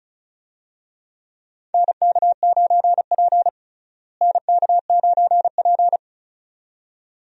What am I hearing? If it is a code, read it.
Morse code, heard as a keyed tone, about 36 wpm, 700 Hz. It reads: NK9P NK9P
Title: USA Call Signs - 4 Characters - Repeated 2x - 35wpm